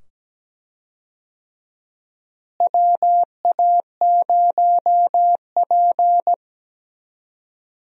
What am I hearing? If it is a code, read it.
Morse code, heard as a keyed tone, about 17 wpm, 700 Hz. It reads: WA0P
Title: USA Call Signs - 4 Characters - Repeated 2x - 17wpm